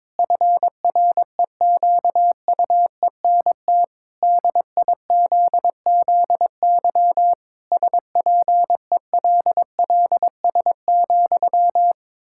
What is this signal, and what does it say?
Morse code, heard as a keyed tone, about 21 wpm, 700 Hz. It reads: FREQUENT DIZZY SPELLS,
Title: Pangram Challenge v2 - 22wpm